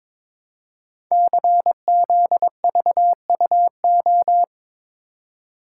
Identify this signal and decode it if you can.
Morse code, heard as a keyed tone, about 22 wpm, 700 Hz. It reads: CZVUO
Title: Code Groups - 5 Letter:Numbers - 22wpm